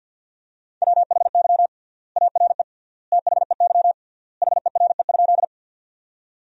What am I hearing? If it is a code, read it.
Morse code, heard as a keyed tone, about 53 wpm, 700 Hz. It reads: WHY ARE THEY HERE?
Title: 3x Word-Spacing - Sentences from Top 200 Words - 50wpm